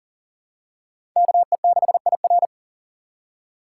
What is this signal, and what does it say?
Morse code, heard as a keyed tone, about 40 wpm, 700 Hz. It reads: KE6IR
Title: USA Call Signs - 2x Character Spacing - 40wpm